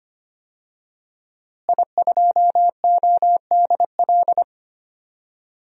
Morse code, heard as a keyed tone, about 25 wpm, 700 Hz. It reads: I2ODL
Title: DXCC-100 Call Signs - 5 Characters - Repeated 3x - 25wpm